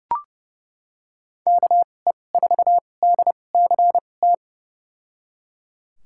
Morse code, heard as a keyed tone, about 30 wpm, spaced as for 22 wpm, 700 Hz. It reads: KE4DCT
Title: USA Call Signs - 2x Character Spacing - 30wpm